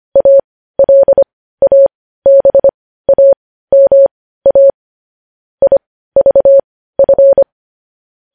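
Morse code, heard as a keyed tone, about 25 wpm, spaced as for 15 wpm, 550 Hz. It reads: ALABAMA IVF
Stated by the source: Morse code news headlines sent at 15.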